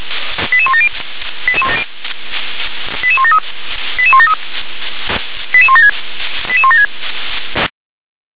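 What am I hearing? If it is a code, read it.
Five-tone selective call (ZVEI), heard as quick runs of five tones, tones of about 80 ms. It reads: 88170 88173 88173 88176 88176
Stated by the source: ZVEI five-tone call received by SDR.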